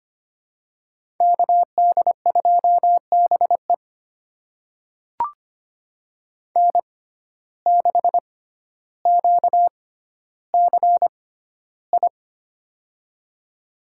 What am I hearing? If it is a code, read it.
Morse code, heard as a keyed tone, about 25 wpm, spaced as for 13 wpm, 700 Hz. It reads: KD2BE N6QCI
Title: USA Call Signs - 5 Characters - 6x Character Spacing - 25wpm